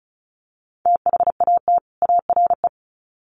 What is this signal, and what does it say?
Morse code, heard as a keyed tone, about 34 wpm, 700 Hz. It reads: THAT ARE
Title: Sets of 2 Words 35wpm